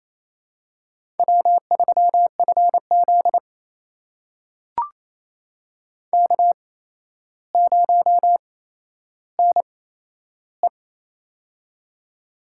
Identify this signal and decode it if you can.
Morse code, heard as a keyed tone, about 28 wpm, spaced as for 12 wpm, 700 Hz. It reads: W3FZ K0NE
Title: USA Call Signs - 4 Characters - 8x Character Spacing - 28wpm